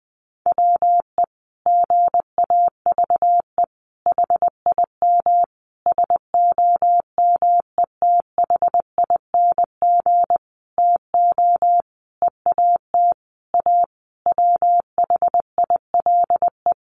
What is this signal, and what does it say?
Morse code, heard as a keyed tone, about 20 wpm, 700 Hz. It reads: WE GAVE HIM SOMETHING TO EAT A WHILE
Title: Sentences from Top 600 Words 20wpm